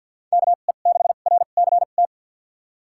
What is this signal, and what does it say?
Morse code, heard as a keyed tone, about 53 wpm, spaced as for 37 wpm, 690 Hz. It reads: KE6RCT
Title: USA Call Signs - 2x Character Spacing - 50wpm